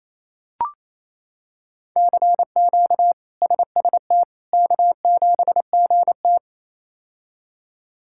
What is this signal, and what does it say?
Morse code, heard as a keyed tone, about 27 wpm, 700 Hz. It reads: CQ SST K7GT
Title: K1USN SST QSO Practice - 28wpm